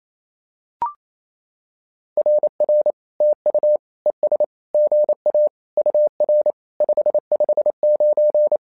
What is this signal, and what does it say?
Morse code, heard as a keyed tone, about 27 wpm, 600 Hz. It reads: RR TU ES GA UR 559
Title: SOTA QSO Practice 28wpm